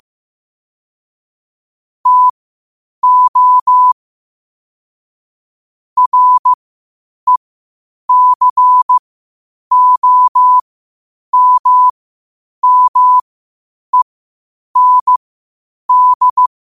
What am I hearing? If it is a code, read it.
Morse code, heard as a keyed tone, about 15 wpm, spaced as for 8 wpm, 1000 Hz. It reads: TO RECOMMEND
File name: qotdfiles / 8wpm